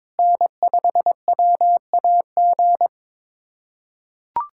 Morse code, heard as a keyed tone, about 22 wpm, 700 Hz. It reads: N5WAG
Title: USA Callsign DE Callsign - 5 Characters - 22wpm